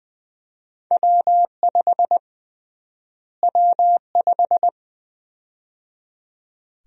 Morse code, heard as a keyed tone, about 20 wpm, 700 Hz. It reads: W5 W5
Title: USA Call Sign Prefixes - Repeated 2x - 20wpm